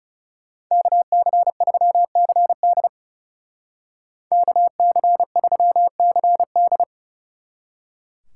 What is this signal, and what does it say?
Morse code, heard as a keyed tone, about 36 wpm, 700 Hz. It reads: KC3CD KC3CD
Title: Speed Racing - USA Call Signs 30wpm